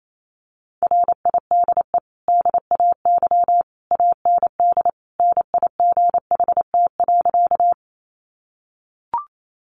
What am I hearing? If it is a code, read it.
Morse code, heard as a keyed tone, about 28 wpm, 700 Hz. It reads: RIDE DAY AND NIGHT.
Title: Sentences from Top 1200 Words 28wpm